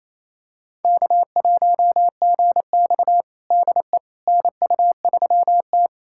Morse code, heard as a keyed tone, about 28 wpm, 700 Hz. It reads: K1GX DE NU3T
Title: USA Callsign DE Callsign - 4 Characters - 28wpm